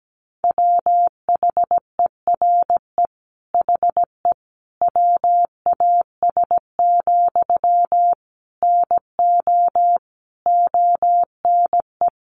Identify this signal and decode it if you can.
Morse code, heard as a keyed tone, about 17 wpm, 700 Hz. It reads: WHERE HE WAS, NO ONE
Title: Sentences from Top 400 Words 17wpm